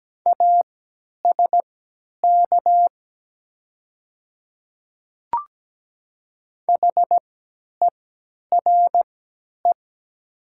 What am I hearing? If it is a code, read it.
Morse code, heard as a keyed tone, about 17 wpm, spaced as for 10 wpm, 700 Hz. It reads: ASK HERE
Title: 3x Character Spacing - Top 200 Words - 17wpm